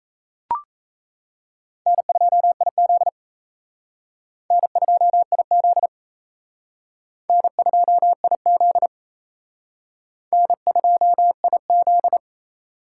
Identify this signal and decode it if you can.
Morse code, heard as a keyed tone, about 34 wpm, 700 Hz. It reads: N2IZ N2IZ N2IZ N2IZ
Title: Speed Racing - USA 1x2 Call Signs 28wpm